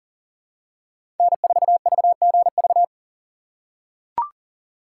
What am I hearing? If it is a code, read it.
Morse code, heard as a keyed tone, about 40 wpm, 700 Hz. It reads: N4VGV
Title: USA Call Signs - 5 Characters - 2x Character Spacing - 40wpm